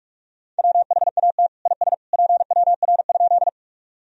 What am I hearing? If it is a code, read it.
Morse code, heard as a keyed tone, about 44 wpm, 700 Hz. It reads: WHAT IS PWR?
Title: Sentences from Top 500 Words and Abbreviations 45wpm